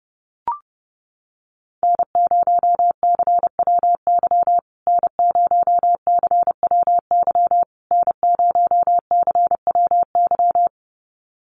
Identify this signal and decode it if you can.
Morse code, heard as a keyed tone, about 30 wpm, 700 Hz. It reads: N0CWY N0CWY N0CWY
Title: USA Call Signs Repeated 3x 30wpm